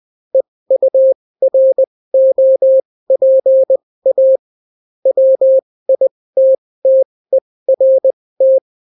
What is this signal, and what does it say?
Morse code, heard as a keyed tone, about 20 wpm, spaced as for 16 wpm, 530 Hz. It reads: EUROPA WITTERT